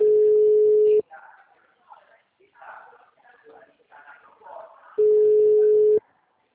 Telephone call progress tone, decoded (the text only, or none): ringback tone